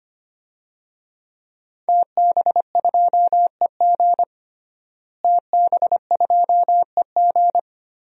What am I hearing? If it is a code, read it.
Morse code, heard as a keyed tone, about 25 wpm, 700 Hz. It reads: TB2EG TB2EG
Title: DXCC-100 Call Signs - 5 Characters - Repeated 3x - 25wpm